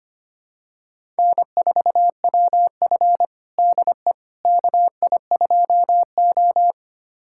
Morse code, heard as a keyed tone, about 25 wpm, 700 Hz. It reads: N4WF DE KI2O
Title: USA Callsign DE Callsign - 4 Characters - 25wpm